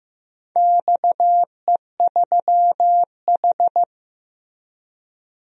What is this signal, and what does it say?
Morse code, heard as a keyed tone, about 15 wpm, 700 Hz. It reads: XE3H
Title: DXCC-100 Call Signs - 4 Characters - Repeated 2x - 15wpm